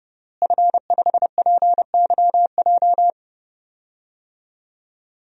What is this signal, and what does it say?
Morse code, heard as a keyed tone, about 29 wpm, 700 Hz. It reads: F5PYJ
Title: DXCC-100 Call Signs - 5 Characters - Repeated 2x - 30wpm